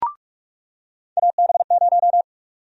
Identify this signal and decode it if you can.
Morse code, heard as a keyed tone, about 46 wpm, 700 Hz. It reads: AB0
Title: USA Call Sign Prefixes - 45wpm